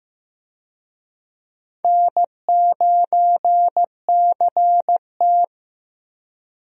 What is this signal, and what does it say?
Morse code, heard as a keyed tone, about 14 wpm, 700 Hz. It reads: N9CT
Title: USA Call Signs - 4 Characters - Repeated 3x - 15wpm